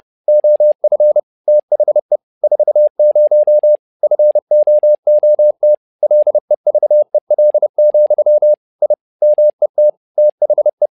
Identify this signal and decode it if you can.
Morse code, heard as a keyed tone, about 30 wpm, 600 Hz. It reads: OF THE 40 FOOT LEVEL, I MET THE
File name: learning center / tower trouble / tower trouble 30wpm